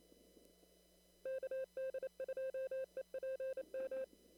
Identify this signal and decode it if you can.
Morse code, heard as a keyed tone, about 27 wpm, 530 Hz. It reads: KD2EPM
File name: CW / kt5j wpx cw-op-k5pi